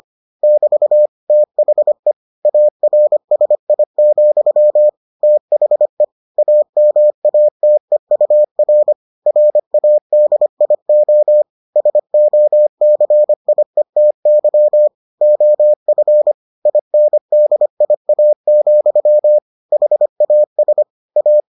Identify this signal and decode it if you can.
Morse code, heard as a keyed tone, about 25 wpm, 600 Hz. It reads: <BT> THE ARSI, THE AMATEUR RADIO SOCIETY OF INDIA, HAS A